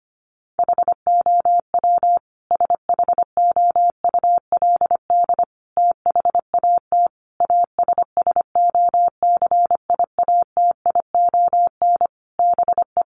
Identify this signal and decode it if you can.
Morse code, heard as a keyed tone, about 25 wpm, 700 Hz. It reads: HOW SHOULD THAT ASSOCIATION BE
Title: Sentences from Top 1300 Words 25wpm